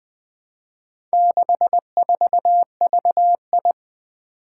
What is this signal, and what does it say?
Morse code, heard as a keyed tone, about 21 wpm, 700 Hz. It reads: 64VI
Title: Four Challenging Letter-Number Combinations 20wpm